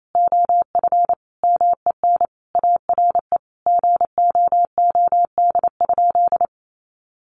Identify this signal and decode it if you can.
Morse code, heard as a keyed tone, about 28 wpm, 700 Hz. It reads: OF MEN ARE GOOD?
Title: Sentences from Top 1300 Words 28wpm